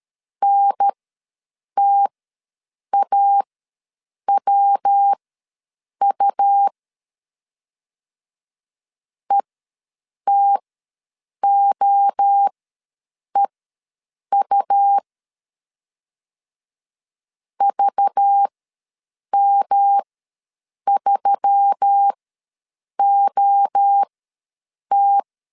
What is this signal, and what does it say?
Morse code, heard as a keyed tone, about 13 wpm, spaced as for 7 wpm, 800 Hz. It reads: NTAWU ETOEU VM3OT
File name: LEZIONI / 5 LBVC / 5 LBV ws 12wpm 3015